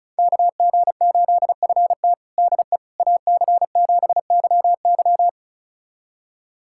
Morse code, heard as a keyed tone, about 34 wpm, 700 Hz. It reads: KG8FT DE AC7YY
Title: USA Callsign DE Callsign - 5 Characters - 35wpm